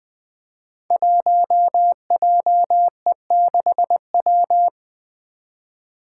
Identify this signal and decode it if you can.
Morse code, heard as a keyed tone, about 20 wpm, 700 Hz. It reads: 1JE6W
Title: Five Challenging Letter-Number Combinations 20wpm